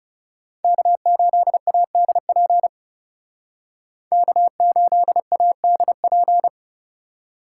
Morse code, heard as a keyed tone, about 38 wpm, 700 Hz. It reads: K8ADP K8ADP